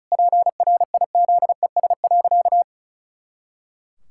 Morse code, heard as a keyed tone, about 36 wpm, 700 Hz. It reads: PRIZES.
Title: Pangram Challenge v2 - 35wpm